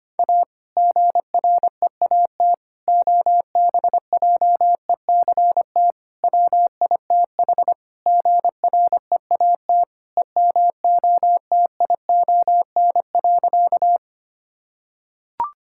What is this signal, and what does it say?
Morse code, heard as a keyed tone, about 25 wpm, 700 Hz. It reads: A GREAT OBJECT WITH GREAT EMOTION.
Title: Sentences from Top 1600 Words 25wpm